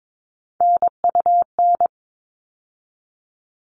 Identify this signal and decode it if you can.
Morse code, heard as a keyed tone, about 22 wpm, 700 Hz. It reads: NUN
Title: All Cut Numbers - 3-Digit 22wpm